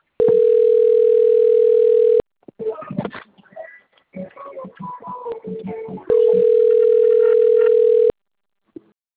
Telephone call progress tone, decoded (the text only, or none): ringback tone